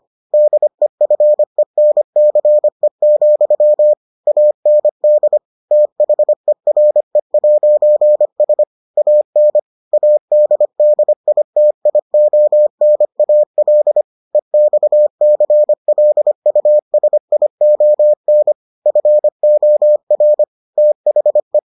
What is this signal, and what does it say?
Morse code, heard as a keyed tone, about 25 wpm, 600 Hz. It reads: DEFENCE, AND THERE'S AN ADDITIONAL EXCLUSION FOR THE